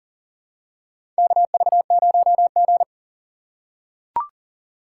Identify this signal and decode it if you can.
Morse code, heard as a keyed tone, about 40 wpm, 700 Hz. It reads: KV0G
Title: USA Call Signs - 4 Characters - 3x Character Spacing - 40wpm